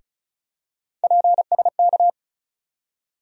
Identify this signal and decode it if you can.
Morse code, heard as a keyed tone, about 36 wpm, 700 Hz. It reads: PSK
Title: USA Call Sign Postfixes - 3 Letters 35wpm